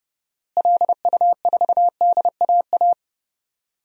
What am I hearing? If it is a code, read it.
Morse code, heard as a keyed tone, about 31 wpm, 700 Hz. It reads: LU4DAA
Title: DXCC-100 Call Signs - 6 Characters - Repeated 3x - 30wpm